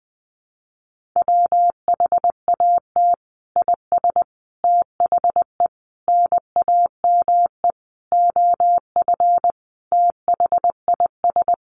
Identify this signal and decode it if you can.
Morse code, heard as a keyed tone, about 20 wpm, 700 Hz. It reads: WHAT IS THE NAME OF THIS
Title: Sentences from Top 1500 Words 20wpm